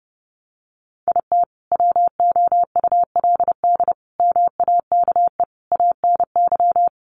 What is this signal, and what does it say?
Morse code, heard as a keyed tone, about 30 wpm, 700 Hz. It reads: IT WOULD MAKE ANY
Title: Sentences from Top 1100 Words 30wpm